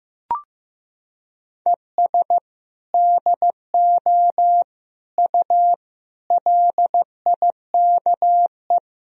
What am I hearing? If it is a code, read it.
Morse code, heard as a keyed tone, about 15 wpm, 700 Hz. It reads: ES DO U LIKE
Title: Sentences from Top 100 Words and Abbreviations 15wpm